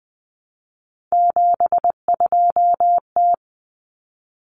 Morse code, heard as a keyed tone, about 20 wpm, 700 Hz. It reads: 72T
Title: Common Cut Numbers - 3-Digit 20wpm